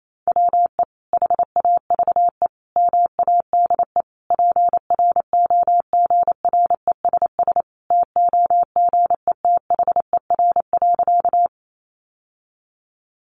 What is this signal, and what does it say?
Morse code, heard as a keyed tone, about 28 wpm, 700 Hz. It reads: WE HAVE MADE PROGRESS TOGETHER.
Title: Sentences from Top 1400 Words 28wpm